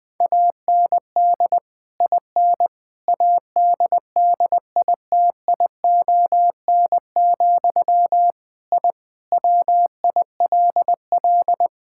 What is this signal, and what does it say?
Morse code, heard as a keyed tone, about 20 wpm, 700 Hz. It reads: AND IN ADDITION, I WILL
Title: Sentences from Top 1000 Words - Review Part 6 - 20wpm